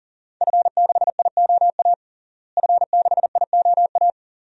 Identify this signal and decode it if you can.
Morse code, heard as a keyed tone, about 37 wpm, 700 Hz. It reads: F6IOA F6IOA
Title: DXCC-100 Call Signs - 5 Characters - Repeated 3x - 40wpm